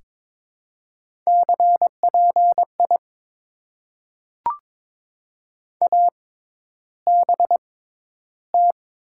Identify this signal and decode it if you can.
Morse code, heard as a keyed tone, about 23 wpm, spaced as for 11 wpm, 700 Hz. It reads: CPI ABT
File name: Top 90 QSO Elements - 6x Character Spacing - 22wpm